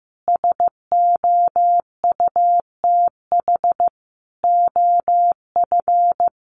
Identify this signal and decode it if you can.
Morse code, heard as a keyed tone, about 15 wpm, 700 Hz. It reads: SOUTH OF
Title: Sentences from Top 1300 Words 15wpm